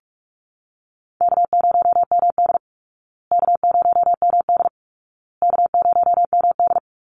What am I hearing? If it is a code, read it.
Morse code, heard as a keyed tone, about 44 wpm, 700 Hz. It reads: K0MD K0MD K0MD
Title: Easy Call Signs Repeated 3x - 45wpm